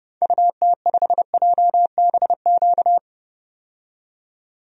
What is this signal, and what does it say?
Morse code, heard as a keyed tone, about 30 wpm, 700 Hz. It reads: UT5JBQ
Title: DXCC-100 Call Signs - 6 Characters - 30wpm